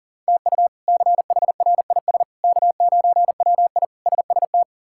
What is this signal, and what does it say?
Morse code, heard as a keyed tone, about 37 wpm, 700 Hz. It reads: TU CHRIS K9WI SST
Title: K1USN SST QSO Practice - 40wpm